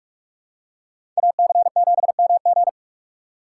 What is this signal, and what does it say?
Morse code, heard as a keyed tone, about 46 wpm, 700 Hz. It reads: AC7MG
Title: USA Call Signs - 2x Character Spacing - 45wpm